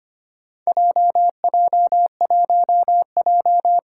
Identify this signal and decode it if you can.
Morse code, heard as a keyed tone, about 25 wpm, 700 Hz. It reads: JJ1J
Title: Four Challenging Letter-Number Combinations 25wpm